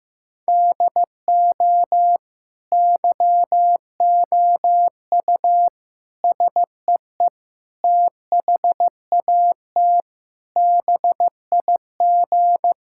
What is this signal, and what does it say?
Morse code, heard as a keyed tone, about 16 wpm, 700 Hz. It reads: DO YOU SEE THAT BIG
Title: Sentences from Top 1000 Words - Review Part 5 - 15wpm